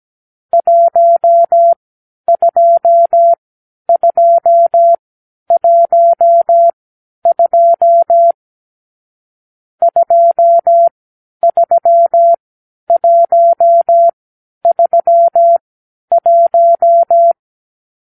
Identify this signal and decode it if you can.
Morse code, heard as a keyed tone, about 17 wpm, spaced as for 10 wpm, 680 Hz. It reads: 12212 23131
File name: k7qo / Lesson 44 (1-3)